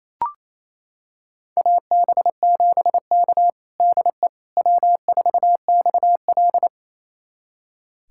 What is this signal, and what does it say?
Morse code, heard as a keyed tone, about 28 wpm, 700 Hz. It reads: AB7K DE W4XL
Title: USA Callsign DE Callsign - 4 Characters - 28wpm